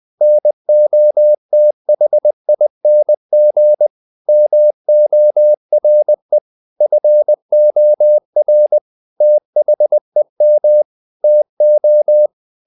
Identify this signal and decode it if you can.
Morse code, heard as a keyed tone, about 20 wpm, 600 Hz. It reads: NOTHING MORE FOR THEM TO
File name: the old man and the sea part 2 20wpm